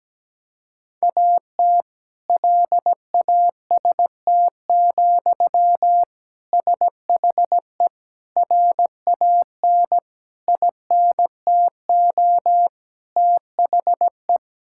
Morse code, heard as a keyed tone, about 17 wpm, 700 Hz. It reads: AT LAST, SHE RAN INTO THE
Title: Sentences from Top 1600 Words 17wpm